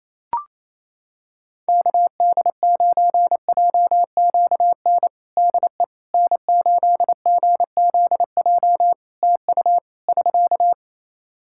Morse code, heard as a keyed tone, about 27 wpm, 700 Hz. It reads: KD9JQN DE N8GZJ TU <SK>
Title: K1USN SST QSO Practice 28wpm